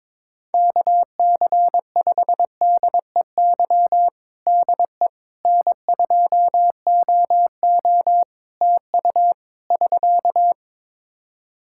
Morse code, heard as a keyed tone, about 22 wpm, 700 Hz. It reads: KC5DEY DE N2OO TU <SK>
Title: K1USN SST QSO Practice 22wpm